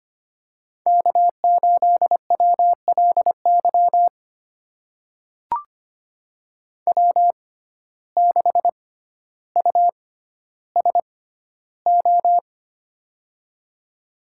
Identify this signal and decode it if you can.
Morse code, heard as a keyed tone, about 25 wpm, spaced as for 13 wpm, 700 Hz. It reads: K8WLY W6USO
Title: USA Call Signs - 5 Characters - 6x Character Spacing - 25wpm